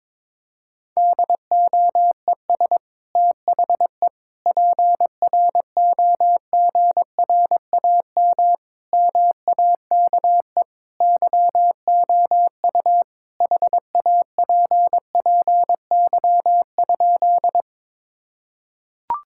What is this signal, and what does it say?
Morse code, heard as a keyed tone, about 22 wpm, 700 Hz. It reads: DOES THE PROGRAM MAKE YOU HAPPY?
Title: Sentences from Top 1000 Words - Review Part 5 - 22wpm